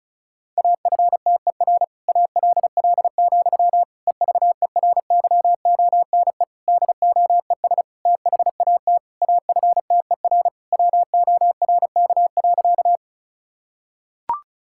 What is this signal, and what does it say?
Morse code, heard as a keyed tone, about 34 wpm, 700 Hz. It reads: AFTER ALL, EVERYONE DOES THAT AFTER WORK.
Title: Sentences from Top 1000 Words - Review Part 5 - 35wpm